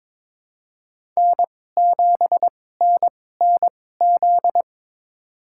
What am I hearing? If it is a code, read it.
Morse code, heard as a keyed tone, about 23 wpm, spaced as for 16 wpm, 700 Hz. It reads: N7NNZ
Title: USA Call Signs - 2x Character Spacing - 22wpm